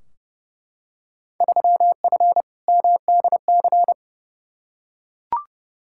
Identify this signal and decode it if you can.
Morse code, heard as a keyed tone, about 32 wpm, 700 Hz. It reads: 3F MDC
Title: Field Day Exchanges Encore - 30wpm